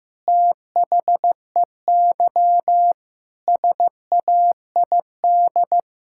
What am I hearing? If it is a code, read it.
Morse code, heard as a keyed tone, about 14 wpm, 700 Hz. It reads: THEY SAID
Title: Sentences from Top 1600 Words 15wpm